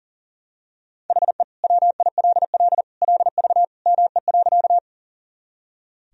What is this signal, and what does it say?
Morse code, heard as a keyed tone, about 40 wpm, 700 Hz. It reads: HE WILL LV ME.
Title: Sentences from Top 400 Words and Abbreviations 40wpm